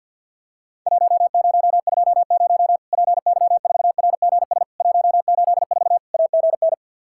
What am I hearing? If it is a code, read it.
Morse code, heard as a keyed tone, about 50 wpm, 700 Hz. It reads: 1020 PY4RGS 184 AGN
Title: Worked All Europe DX Contest - QTC Exchange with Repeats - 50wpm